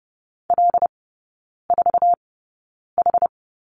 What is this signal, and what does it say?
Morse code, heard as a keyed tone, about 30 wpm, spaced as for 9 wpm, 700 Hz. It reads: L4H
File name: 081 Lesson 14e - Number 2 - Three Characters 30wpm